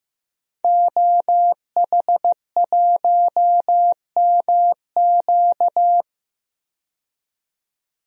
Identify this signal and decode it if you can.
Morse code, heard as a keyed tone, about 15 wpm, 700 Hz. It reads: OH1MQ
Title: DXCC-100 Call Signs - 5 Characters - 15wpm